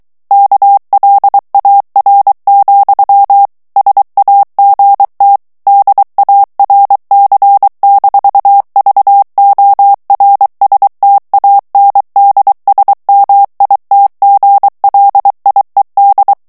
Morse code, heard as a keyed tone, about 23 wpm, 800 Hz. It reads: KLAR, SAGT DARC-VORSTANDSMITGLIED
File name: morse mp3 / ALLEMAND texte clair 1 vitesse 22 mots minute